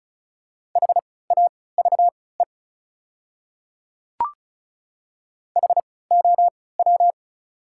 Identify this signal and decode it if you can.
Morse code, heard as a keyed tone, about 34 wpm, spaced as for 20 wpm, 690 Hz. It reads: HAVE HOW
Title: 3x Character Spacing - Top 100 Words - 35wpm